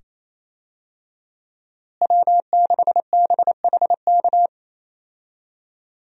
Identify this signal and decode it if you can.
Morse code, heard as a keyed tone, about 28 wpm, 700 Hz. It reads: W6BHK
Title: USA Call Signs - 5 Characters - Repeated 2x - 28wpm